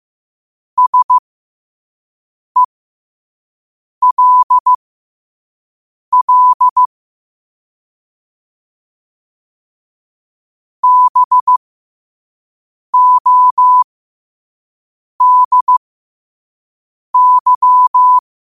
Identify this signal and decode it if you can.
Morse code, heard as a keyed tone, about 15 wpm, spaced as for 5 wpm, 1000 Hz. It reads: SELL BODY
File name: qotdfiles / 5wpm